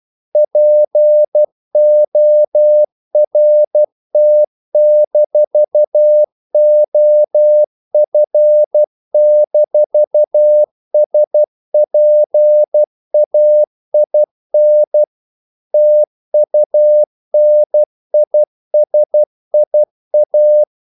Morse code, heard as a keyed tone, about 12 wpm, 600 Hz. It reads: PORT-OF-SPAIN TUNISIA
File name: CW / Practice / NationalCapitals12